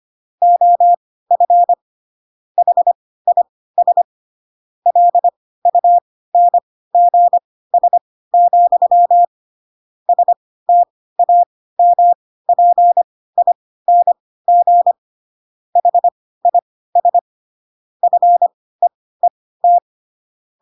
Morse code, heard as a keyed tone, about 25 wpm, spaced as for 16 wpm, 700 Hz. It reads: OF HIS LUNGS, STAMPING HIS FEET